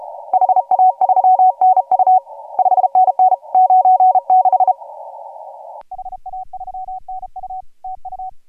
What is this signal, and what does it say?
Morse code, heard as a keyed tone, about 37 wpm, 750 Hz. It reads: HA3NU 5NN 96 HA3NU TU
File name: AUDIO2018 / HG1G CW